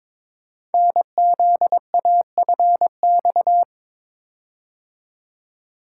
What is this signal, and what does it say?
Morse code, heard as a keyed tone, about 22 wpm, 700 Hz. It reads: NZAFX
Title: Code Groups - 5 Letters - 22wpm